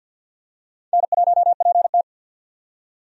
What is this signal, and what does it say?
Morse code, heard as a keyed tone, about 47 wpm, 700 Hz. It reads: N1PT
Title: USA Call Signs - 4 Characters - 50wpm